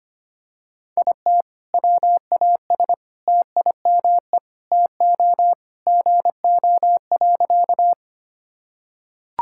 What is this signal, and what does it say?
Morse code, heard as a keyed tone, about 25 wpm, 700 Hz. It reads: IT WAS TIME TO GO.
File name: practice / 25wpm / Sentences with 4-Word Phrases - 25wpm